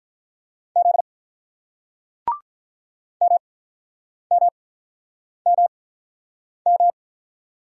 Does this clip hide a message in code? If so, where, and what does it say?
Yes: Morse code from 0.8 to 6.9 s, heard as a keyed tone, about 47 wpm, 700 Hz. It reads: Z M M M M